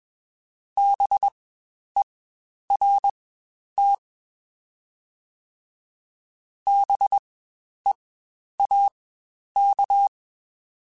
Morse code, heard as a keyed tone, about 21 wpm, spaced as for 10 wpm, 800 Hz. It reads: BERT BEAK
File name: Spaced Alphas / Spaced Alphas mp3s / Bs sp @20